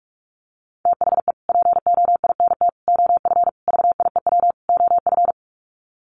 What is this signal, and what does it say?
Morse code, heard as a keyed tone, about 44 wpm, 700 Hz. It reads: THE POINT OF VIEW OF
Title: Sets of 5 Words 45wpm